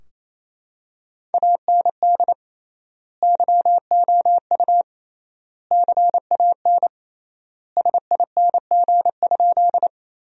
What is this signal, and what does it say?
Morse code, heard as a keyed tone, about 28 wpm, 700 Hz. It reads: AND YOU CAN SING?
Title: 3x Word-Spacing - Sentences from Top 400 Words - 28wpm